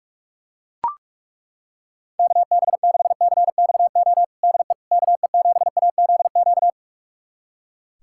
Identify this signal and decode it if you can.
Morse code, heard as a keyed tone, about 41 wpm, 700 Hz. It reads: KB6CXO DE KE7AZQ
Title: USA Callsign DE Callsign - 6 Characters - 45wpm